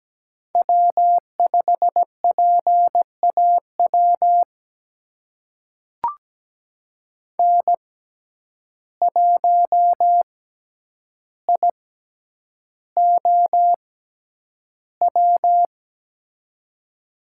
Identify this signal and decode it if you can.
Morse code, heard as a keyed tone, about 17 wpm, spaced as for 9 wpm, 700 Hz. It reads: W5PAW N1IOW
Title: USA Call Signs - 5 Characters - 6x Character Spacing - 17wpm